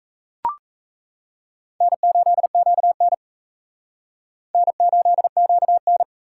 Morse code, heard as a keyed tone, about 40 wpm, 700 Hz. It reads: N8QN N8QN